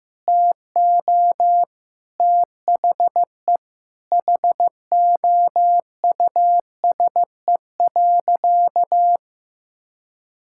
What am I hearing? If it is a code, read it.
Morse code, heard as a keyed tone, about 15 wpm, 700 Hz. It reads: TO THE HOUSE.
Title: Sentences from Top 1000 Words - Review Part 6 - 15wpm